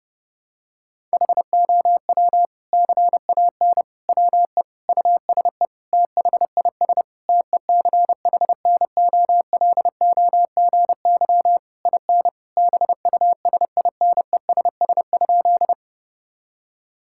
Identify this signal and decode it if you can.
Morse code, heard as a keyed tone, about 32 wpm, 700 Hz. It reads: HOW CAN WE USE THIS TECHNOLOGY IN BUSINESS?
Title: Sentences from Top 1000 Words - Review Part 5 - 30wpm